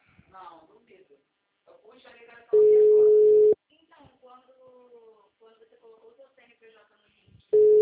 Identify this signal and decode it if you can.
Ringback tone (Europe, 425 Hz), with speech.